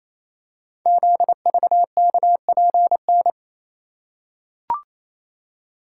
Morse code, heard as a keyed tone, about 28 wpm, 690 Hz. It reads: ZVKPN
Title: Code Groups - 5 Letters - 28wpm